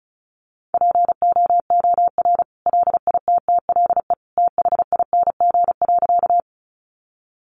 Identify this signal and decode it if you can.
Morse code, heard as a keyed tone, about 34 wpm, 700 Hz. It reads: POOR LITTLE THING.
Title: Sentences from Top 1000 Words 35wpm